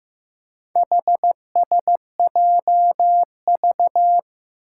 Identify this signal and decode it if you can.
Morse code, heard as a keyed tone, about 15 wpm, 700 Hz. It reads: HSJV
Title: Four Challenging Letter-Number Combinations 15wpm